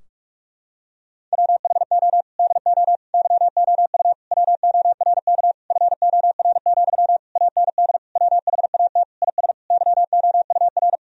Morse code, heard as a keyed tone, about 46 wpm, 700 Hz. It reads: WHO DO YOU WORK FOR, AND WHAT IS YOUR
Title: Sentences from Top 1000 Words - Review Part 4 - 45wpm